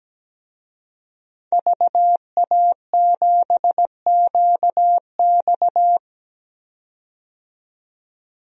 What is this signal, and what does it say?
Morse code, heard as a keyed tone, about 17 wpm, 700 Hz. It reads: VA7QX
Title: DXCC-100 Call Signs - 5 Characters - 17wpm